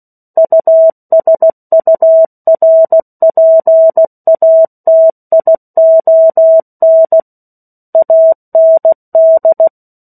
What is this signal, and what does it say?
Morse code, heard as a keyed tone, about 16 wpm, 650 Hz. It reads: USURPATION AND
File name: 03-Chapter-03 16 WPM